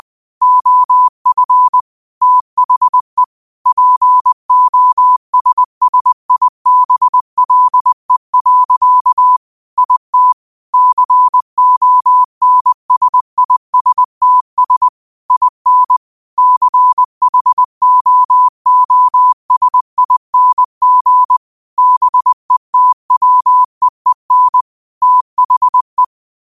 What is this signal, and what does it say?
Morse code, heard as a keyed tone, about 20 wpm, 1000 Hz. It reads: OF THE POSSIBLE. IT CONSISTS IN CHOOSING BETWEEN THE
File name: qotdfiles / 20wpm